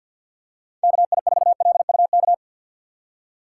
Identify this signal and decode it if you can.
Morse code, heard as a keyed tone, about 53 wpm, 700 Hz. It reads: KI4LUK
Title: USA Call Signs - 6 Characters - 3x Character Spacing - 50wpm